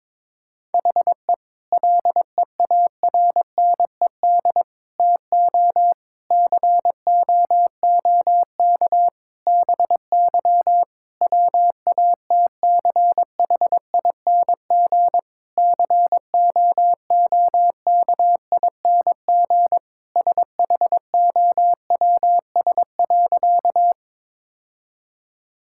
Morse code, heard as a keyed tone, about 22 wpm, 700 Hz. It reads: HE LEARNED TO COOK BY WATCHING COOKING SHOWS.
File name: Difficult Sentences with Hints - 22wpm